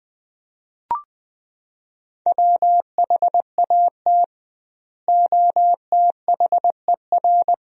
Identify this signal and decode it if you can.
Morse code, heard as a keyed tone, about 20 wpm, 700 Hz. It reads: WHAT OTHER
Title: 2x Word-Spacing - Sentences from Top 100 Words - 20wpm